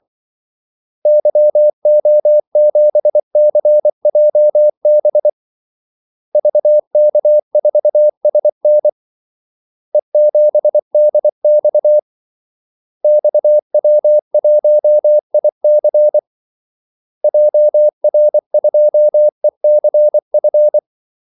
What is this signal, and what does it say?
Morse code, heard as a keyed tone, about 24 wpm, 600 Hz. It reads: YO7CJB VK4SN E7DX XW1IC JR2ECF
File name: Callsigns-24wpm